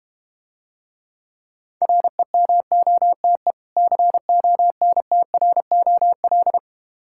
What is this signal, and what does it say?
Morse code, heard as a keyed tone, about 32 wpm, 700 Hz. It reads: REMOTE CONTROL